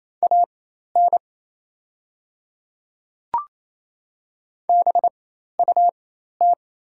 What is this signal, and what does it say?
Morse code, heard as a keyed tone, about 28 wpm, spaced as for 13 wpm, 700 Hz. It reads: AN BUT
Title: 4x Character Spacing - Top 100 Words - 28wpm